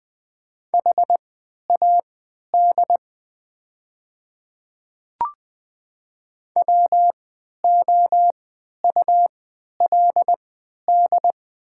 Morse code, heard as a keyed tone, about 20 wpm, spaced as for 11 wpm, 690 Hz. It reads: HAD WOULD
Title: 3x Character Spacing - Top 300 Words - 20wpm